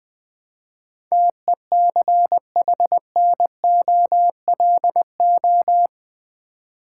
Morse code, heard as a keyed tone, about 20 wpm, 700 Hz. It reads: TECHNOLO
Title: Build A Word - 4 Letters At A Time - 10-Letter Words - 20wpm